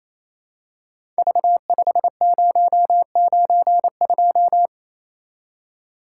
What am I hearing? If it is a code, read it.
Morse code, heard as a keyed tone, about 28 wpm, 700 Hz. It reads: V5092
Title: Code Groups - 5 Letter:Numbers - 28wpm